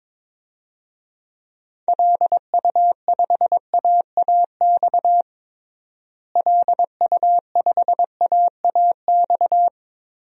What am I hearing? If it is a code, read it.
Morse code, heard as a keyed tone, about 22 wpm, 700 Hz. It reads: LU5AAX LU5AAX
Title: DXCC-100 Call Signs - 6 Characters - Repeated 3x - 22wpm